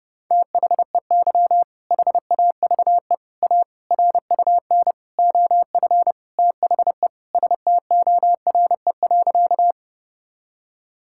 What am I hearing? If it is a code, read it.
Morse code, heard as a keyed tone, about 30 wpm, 700 Hz. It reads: THEY HAVE A RUN OF THE STORE.
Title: Sentences from Top 1000 Words - Review Part 5 - 30wpm